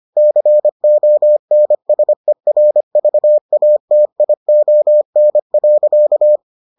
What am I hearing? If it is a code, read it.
Morse code, heard as a keyed tone, about 25 wpm, 600 Hz. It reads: CONSERVATION.